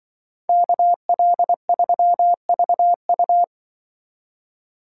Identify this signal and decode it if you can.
Morse code, heard as a keyed tone, about 24 wpm, 700 Hz. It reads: KL3VU
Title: Speed Racing - USA 2x2 Call Signs 20wpm